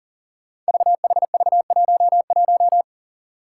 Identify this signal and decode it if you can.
Morse code, heard as a keyed tone, about 41 wpm, 700 Hz. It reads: VHV11
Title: Five Challenging Letter-Number Combinations 40wpm